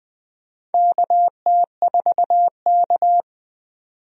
Morse code, heard as a keyed tone, about 20 wpm, 700 Hz. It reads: KT4K
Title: USA Call Signs - 4 Characters - 20wpm